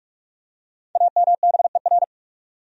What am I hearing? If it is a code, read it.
Morse code, heard as a keyed tone, about 48 wpm, 700 Hz. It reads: AMBER
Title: USA Common Names - 45wpm